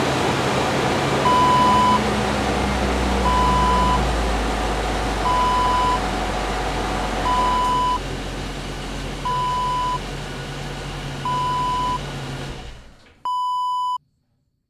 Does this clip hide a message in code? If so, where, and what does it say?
No.